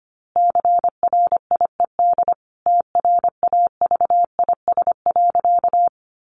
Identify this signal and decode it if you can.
Morse code, heard as a keyed tone, about 25 wpm, 700 Hz. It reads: CRIED TRAVIS.
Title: A Sound Of Thunder 25wpm